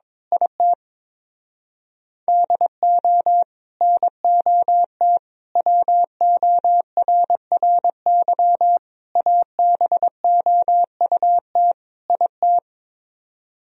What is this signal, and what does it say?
Morse code, heard as a keyed tone, about 23 wpm, 700 Hz. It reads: IT DO NOT WORRY ABOUT IT
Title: Speed-Racing - Sets of 5 Words 22wpm